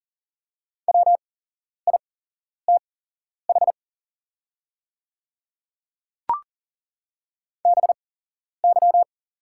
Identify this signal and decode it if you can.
Morse code, heard as a keyed tone, about 40 wpm, spaced as for 11 wpm, 700 Hz. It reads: WITH BY
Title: 8x Character Spacing - Top 100 Words - 40wpm